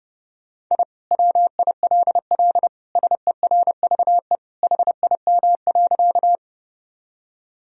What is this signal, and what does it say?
Morse code, heard as a keyed tone, about 30 wpm, 700 Hz. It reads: I WILL SERVE HIM.
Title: Sentences from Top 1000 Words - Review Part 2 - 30wpm